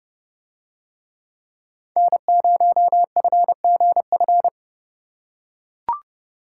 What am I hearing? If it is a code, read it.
Morse code, heard as a keyed tone, about 28 wpm, 700 Hz. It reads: N0FGF